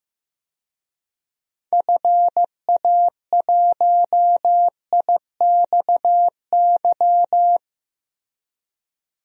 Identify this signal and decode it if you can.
Morse code, heard as a keyed tone, about 15 wpm, 700 Hz. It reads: FA1IXY
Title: DXCC-100 Call Signs - 6 Characters - Repeated 2x - 15wpm